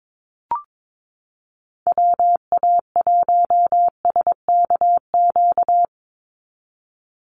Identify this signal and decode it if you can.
Morse code, heard as a keyed tone, about 22 wpm, 700 Hz. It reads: WA1SKQ
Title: Call Signs Repeated 2x - 22wpm